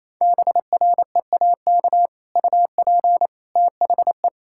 Morse code, heard as a keyed tone, about 28 wpm, 700 Hz. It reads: BREAK UP THE
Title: Sentences from Top 1600 Words 28wpm